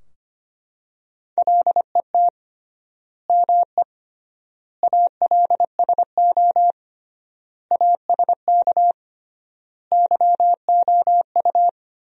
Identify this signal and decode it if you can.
Morse code, heard as a keyed tone, about 25 wpm, 700 Hz. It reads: LET ME ALSO ASK YOU
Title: 3x Word-Spacing - Sentences from Top 400 Words - 25wpm